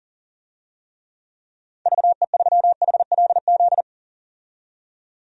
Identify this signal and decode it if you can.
Morse code, heard as a keyed tone, about 40 wpm, 700 Hz. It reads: VE3HLZ
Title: DXCC-100 Call Signs - 6 Characters - 40wpm